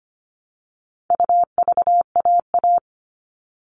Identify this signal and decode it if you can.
Morse code, heard as a keyed tone, about 25 wpm, 700 Hz. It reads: UVAA
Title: All Cut Numbers - 4-Digit 25wpm